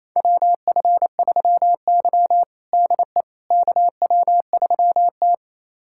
Morse code, heard as a keyed tone, about 28 wpm, 700 Hz. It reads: WF3Y DE KW3T
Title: USA Callsign DE Callsign - 4 Characters - 28wpm